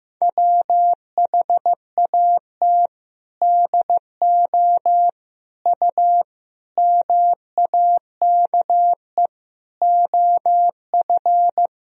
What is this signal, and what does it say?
Morse code, heard as a keyed tone, about 15 wpm, 700 Hz. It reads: WHAT DO U MAKE OF
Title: Sentences from Top 100 Words and Abbreviations 15wpm